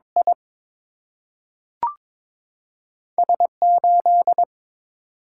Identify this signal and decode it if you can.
Morse code, heard as a keyed tone, about 22 wpm, 700 Hz. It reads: I S 8
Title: Single-Letter Single-Number 22wpm